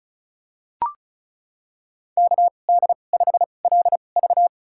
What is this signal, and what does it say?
Morse code, heard as a keyed tone, about 34 wpm, spaced as for 25 wpm, 700 Hz. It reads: KD5LV
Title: USA Call Signs - 5 Characters - 2x Character Spacing - 35wpm